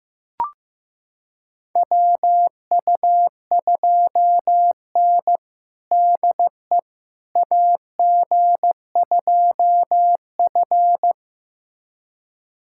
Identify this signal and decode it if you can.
Morse code, heard as a keyed tone, about 15 wpm, 700 Hz. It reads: WU2N DE AG2F
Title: USA Callsign DE Callsign - 4 Characters - 15wpm